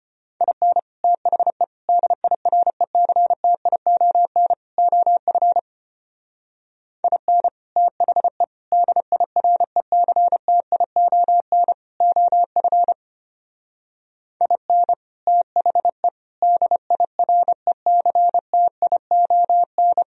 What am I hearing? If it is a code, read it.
Morse code, heard as a keyed tone, about 33 wpm, 700 Hz. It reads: IN THE DIRECTION OF IN THE DIRECTION OF IN THE DIRECTION